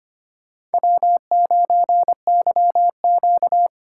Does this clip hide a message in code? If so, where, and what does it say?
Yes: Morse code from 0.7 to 3.7 s, heard as a keyed tone, about 32 wpm, 700 Hz. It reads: W9YQ